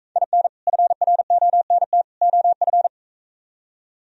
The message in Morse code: IN FRONT OF

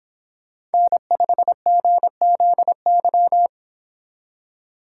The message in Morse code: N5GZY

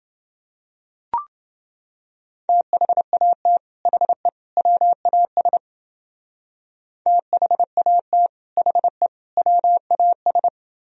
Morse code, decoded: THAT HE WAS THAT HE WAS

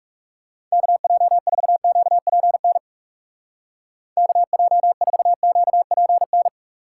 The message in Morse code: KJ4QPN KJ4QPN

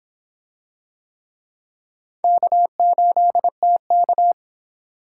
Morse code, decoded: K8TK

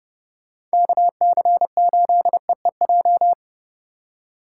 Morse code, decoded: KC8EEJ

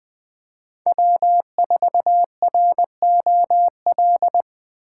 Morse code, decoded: W4ROL

